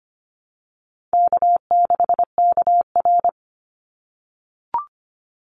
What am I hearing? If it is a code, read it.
Morse code, heard as a keyed tone, about 25 wpm, 700 Hz. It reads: K6KR
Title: Easy Call Signs - 25wpm